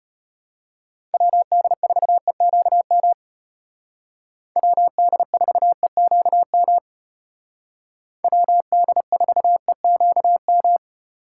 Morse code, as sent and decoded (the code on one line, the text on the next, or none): .-- -.. ....- . --.- -- / .-- -.. ....- . --.- -- / .-- -.. ....- . --.- --
WD4EQM WD4EQM WD4EQM